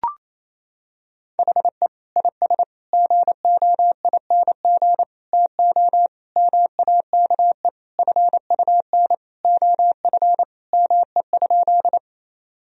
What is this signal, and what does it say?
Morse code, heard as a keyed tone, about 28 wpm, 700 Hz. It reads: HE IS GOING TO MAKE FUN OF ME?
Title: Sentences from Top 1000 Words - Review Part 4 - 28wpm